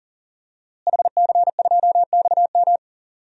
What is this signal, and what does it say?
Morse code, heard as a keyed tone, about 40 wpm, 700 Hz. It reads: HC2XM
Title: DXCC-100 Call Signs - 5 Characters - Repeated 2x - 40wpm